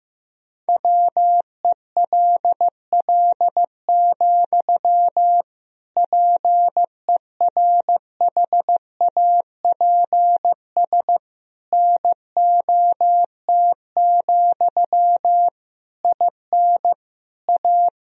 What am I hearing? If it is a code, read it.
Morse code, heard as a keyed tone, about 15 wpm, 700 Hz. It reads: WELL, PERHAPS NOT, IN A
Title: Sentences from Top 1000 Words - Review Part 2 - 15wpm